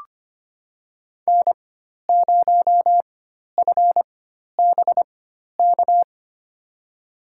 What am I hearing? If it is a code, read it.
Morse code, heard as a keyed tone, about 25 wpm, spaced as for 12 wpm, 700 Hz. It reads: N0FBK
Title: USA Call Signs - 4x Character Spacing - 25wpm